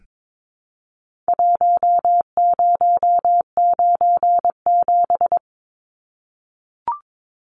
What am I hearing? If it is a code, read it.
Morse code, heard as a keyed tone, about 22 wpm, 700 Hz. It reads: 1097
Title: Numbers - Contest Style - 4-Digit 22wpm